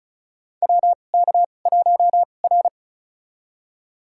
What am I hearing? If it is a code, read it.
Morse code, heard as a keyed tone, about 34 wpm, spaced as for 25 wpm, 700 Hz. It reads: WK1R
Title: USA Call Signs - 2x Character Spacing - 35wpm